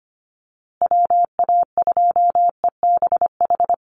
Morse code, heard as a keyed tone, about 25 wpm, 700 Hz. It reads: WA2EBH
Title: USA Call Signs Repeated 2x 25wpm